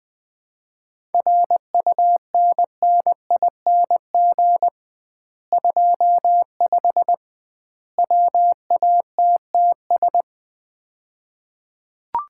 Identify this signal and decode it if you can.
Morse code, heard as a keyed tone, about 21 wpm, 700 Hz. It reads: RUNNING 25 WATTS
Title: Common QSO Phrases - 2x Word-Spacing - 20wpm